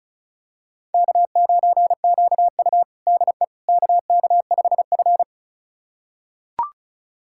Morse code, heard as a keyed tone, about 36 wpm, 700 Hz. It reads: K9QU DE KK5F
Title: USA Callsign DE Callsign - 4 Characters - 35wpm